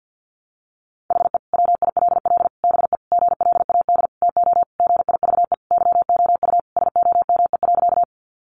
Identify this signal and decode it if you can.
Morse code, heard as a keyed tone, about 50 wpm, 700 Hz. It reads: HE WILL BE GLAD TO GIVE YOU SOME.
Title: Sentences from Top 800 Words 50wpm